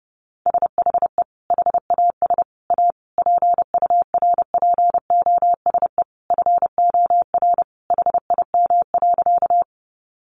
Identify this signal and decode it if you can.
Morse code, heard as a keyed tone, about 30 wpm, 700 Hz. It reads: SHE HAS A PURPOSE FOR HIM.
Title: Sentences from Top 800 Words 30wpm